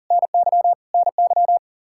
Morse code, heard as a keyed tone, about 40 wpm, 700 Hz. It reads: NY NY
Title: Practice QSO - Exchange 1 - Beginning - 40wpm